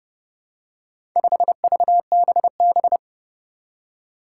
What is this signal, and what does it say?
Morse code, heard as a keyed tone, about 30 wpm, 700 Hz. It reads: 5VBB
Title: Four Challenging Letter-Number Combinations 30wpm